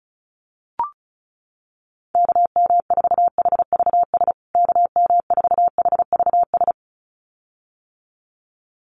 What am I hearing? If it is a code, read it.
Morse code, heard as a keyed tone, about 34 wpm, 700 Hz. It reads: KM4HVS KM4HVS
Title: USA Call Signs Repeated 2x 35wpm